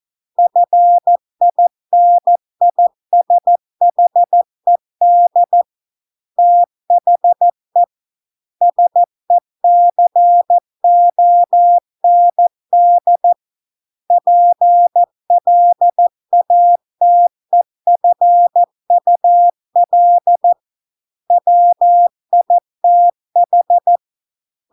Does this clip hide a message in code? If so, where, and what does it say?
Yes: Morse code from 0.4 to 24.0 s, heard as a keyed tone, about 14 wpm, 700 Hz. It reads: FINISHED THE SECOND PLATEFUL WITH